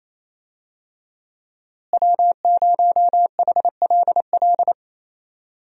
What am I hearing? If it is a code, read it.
Morse code, heard as a keyed tone, about 27 wpm, 700 Hz. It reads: W0HLL
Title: USA Call Signs - 5 Characters - Repeated 2x - 28wpm